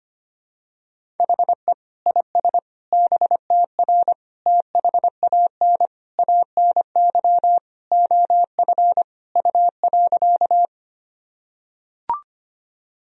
Morse code, heard as a keyed tone, about 25 wpm, 700 Hz. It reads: HE IS BTR THAN ANY OF U.
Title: Sentences from Top 400 Words and Abbreviations 25wpm